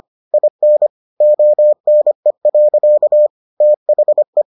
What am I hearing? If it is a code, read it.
Morse code, heard as a keyed tone, about 25 wpm, 600 Hz. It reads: IN ONE. THE